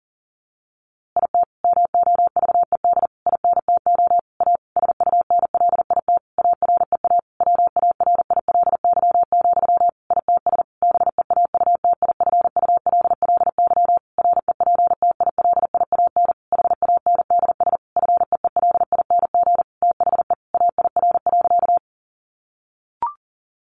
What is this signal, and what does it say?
Morse code, heard as a keyed tone, about 40 wpm, 700 Hz. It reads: IT MOVED INTO A SUNLIT AREA WARILY, ITS BEAUTIFULLY REPTILIAN HANDS FEELING THE AIR.
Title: A Sound Of Thunder 40wpm